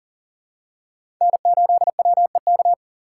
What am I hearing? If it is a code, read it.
Morse code, heard as a keyed tone, about 40 wpm, 700 Hz. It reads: N8WEK
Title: USA Call Signs - 5 Characters - 6x Character Spacing - 40wpm